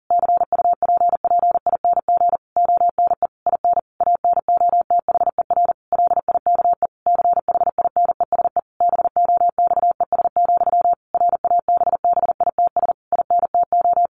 Morse code, heard as a keyed tone, about 40 wpm, 700 Hz. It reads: CUPPING ONE IN ANOTHER LIKE CHINESE BOXES, RABBITS INTO
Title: A Sound Of Thunder 40wpm